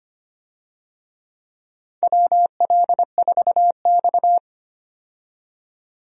Morse code, heard as a keyed tone, about 25 wpm, 700 Hz. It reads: WL4X
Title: USA Call Signs - 4 Characters - 25wpm